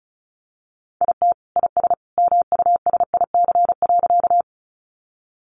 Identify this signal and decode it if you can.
Morse code, heard as a keyed tone, about 34 wpm, 700 Hz. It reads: IT IS MUSIC.
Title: Sentences from Top 300 Words 35wpm